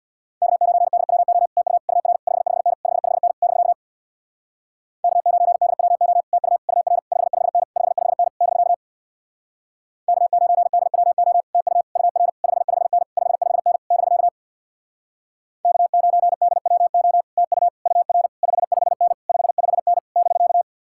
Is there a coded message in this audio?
Yes: Morse code from 0.4 to 20.6 s, heard as a keyed tone, about 66 wpm, 700 Hz. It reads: K9DWO TU UR 55N 55N <BK> K9DWO TU UR 55N 55N <BK> K9DWO TU UR 55N 55N <BK> K9DWO TU UR 55N 55N <BK>